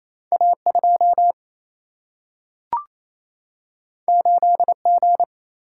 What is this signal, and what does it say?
Morse code, heard as a keyed tone, about 28 wpm, 700 Hz. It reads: A 2 8 G
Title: Single-Letter Single-Number 28wpm